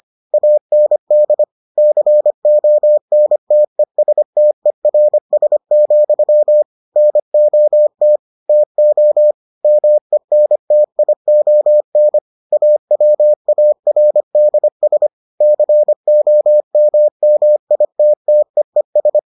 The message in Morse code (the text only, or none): AND CONTESTERS, NOT TO MENTION AWARDS COMMITTEES